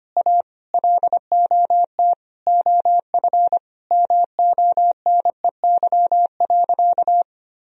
Morse code, heard as a keyed tone, about 25 wpm, 700 Hz. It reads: A LOT OF MONEY.
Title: Sentences from Top 1000 Words - Review Part 4 - 25wpm